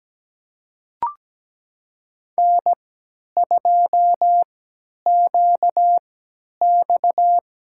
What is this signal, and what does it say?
Morse code, heard as a keyed tone, about 17 wpm, spaced as for 10 wpm, 700 Hz. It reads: N2QX
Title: USA Call Signs - 5 Characters - 3x Character Spacing - 17wpm